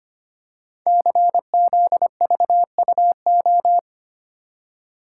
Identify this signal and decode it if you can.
Morse code, heard as a keyed tone, about 25 wpm, 700 Hz. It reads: CZVUO
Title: Code Groups - 5 Letter:Numbers - 25wpm